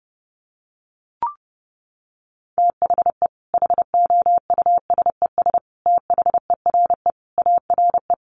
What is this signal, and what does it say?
Morse code, heard as a keyed tone, about 30 wpm, 700 Hz. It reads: THE HOUSES THERE ARE
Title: Easy Phrases 30wpm